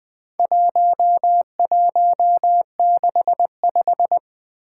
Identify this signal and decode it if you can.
Morse code, heard as a keyed tone, about 20 wpm, 700 Hz. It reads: 1165
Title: Four Challenging Letter-Number Combinations 20wpm